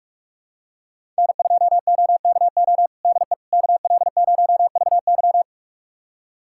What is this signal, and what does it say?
Morse code, heard as a keyed tone, about 46 wpm, 690 Hz. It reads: N2OKO DE KL0VY
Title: USA Callsign DE Callsign 45wpm